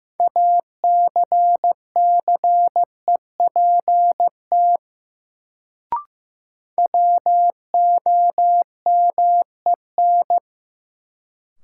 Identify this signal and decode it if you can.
Morse code, heard as a keyed tone, about 15 wpm, 700 Hz. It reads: ACCEPT WOMEN
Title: Sentences from Top 1300 Words - New Word Drill Alt - 15wpm